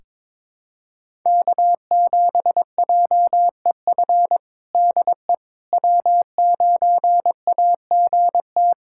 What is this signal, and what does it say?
Morse code, heard as a keyed tone, about 22 wpm, 700 Hz. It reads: K7JEF DE W9AGT
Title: USA Callsign DE Callsign - 5 Characters - 22wpm